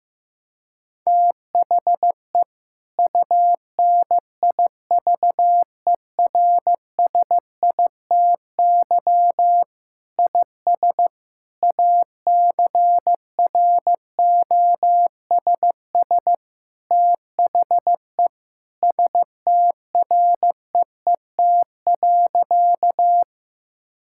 Morse code, heard as a keyed tone, about 16 wpm, 700 Hz. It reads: THE UNIVERSITY IS ACROSS THE STREET.
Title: Sentences from Top 1000 Words - Review Part 5 - 15wpm